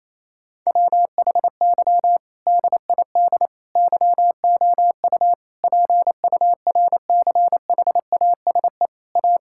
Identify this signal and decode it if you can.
Morse code, heard as a keyed tone, about 28 wpm, 700 Hz. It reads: WHY DID YOU PURCHASE A
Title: Sentences from Top 1000 Words - Review Part 6 - 28wpm